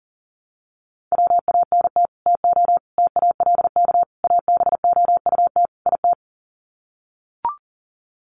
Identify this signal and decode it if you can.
Morse code, heard as a keyed tone, about 40 wpm, 700 Hz. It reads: WANT TO TALK ABOUT IT
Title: Sets of 5 Words Encore 40wpm